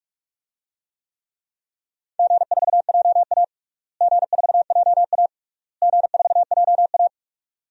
Morse code, heard as a keyed tone, about 42 wpm, 700 Hz. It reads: G4JA G4JA G4JA
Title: DXCC-100 Call Signs - 4 Characters - Repeated 3x - 45wpm